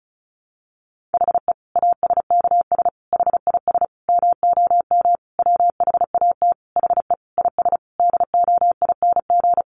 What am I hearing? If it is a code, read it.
Morse code, heard as a keyed tone, about 34 wpm, 700 Hz. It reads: HE ASKS HIS MOM WHAT HE IS DOING
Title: Easy Phrases 35wpm